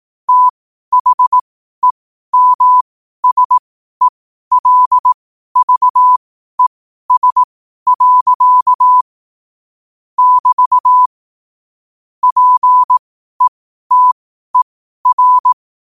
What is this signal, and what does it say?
Morse code, heard as a keyed tone, about 18 wpm, spaced as for 12 wpm, 1000 Hz. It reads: THEMSELVES. <BT> PETER
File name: qotdfiles / 12wpm